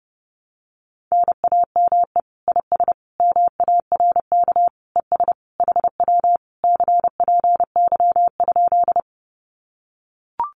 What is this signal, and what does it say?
Morse code, heard as a keyed tone, about 30 wpm, 700 Hz. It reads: NAME IS MARK ES HW CPY?
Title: Common QSO Phrases Encore 30wpm